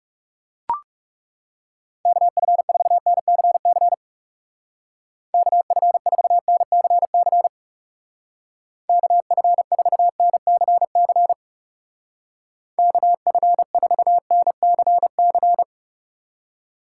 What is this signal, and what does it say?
Morse code, heard as a keyed tone, about 37 wpm, 700 Hz. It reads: KF4NCC KF4NCC KF4NCC KF4NCC
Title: Speed Racing - USA 2x3 Call Signs 30wpm